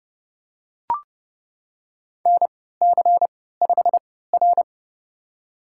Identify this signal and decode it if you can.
Morse code, heard as a keyed tone, about 29 wpm, spaced as for 17 wpm, 700 Hz. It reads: NC5R
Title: USA Call Signs - 4 Characters - 3x Character Spacing - 30wpm